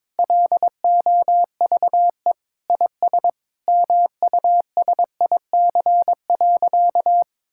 Morse code, heard as a keyed tone, about 22 wpm, 700 Hz. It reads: LOVE IS MUSIC.
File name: practice / 22wpm / Sentences with 3-Word Phrases - 22wpm